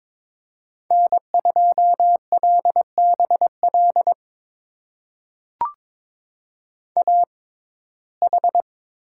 Morse code, heard as a keyed tone, about 22 wpm, spaced as for 16 wpm, 700 Hz. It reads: N2LBL AH
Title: USA Call Signs - 6x Character Spacing - 22wpm